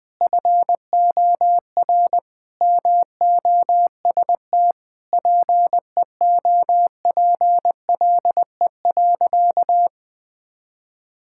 Morse code, heard as a keyed tone, about 21 wpm, 700 Hz. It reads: FOR MOST PEOPLE.
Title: Sentences with 6-Word Phrases - 20wpm